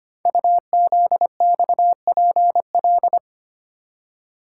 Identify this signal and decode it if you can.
Morse code, heard as a keyed tone, about 25 wpm, 700 Hz. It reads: UZXPL
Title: Code Groups - 5 Letters - 25wpm